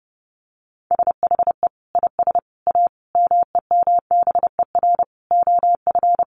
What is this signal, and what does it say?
Morse code, heard as a keyed tone, about 30 wpm, 700 Hz. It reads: SHE IS A MEMBER OF
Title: Sentences from Top 1100 Words 30wpm